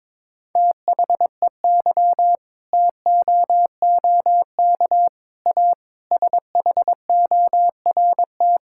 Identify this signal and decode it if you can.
Morse code, heard as a keyed tone, about 22 wpm, 700 Hz. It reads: THEY TOOK A SHORT
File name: Difficult Sentences with Hints - 22wpm